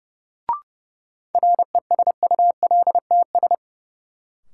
Morse code, heard as a keyed tone, about 30 wpm, 700 Hz. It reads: RESULTS
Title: Sets of 4 Words - New Word Drill - 30wpm